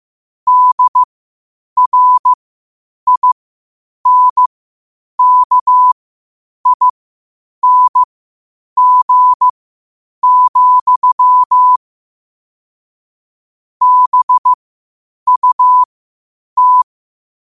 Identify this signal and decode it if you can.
Morse code, heard as a keyed tone, about 15 wpm, spaced as for 8 wpm, 1000 Hz. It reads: DRINKING, BUT
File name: qotdfiles / 8wpm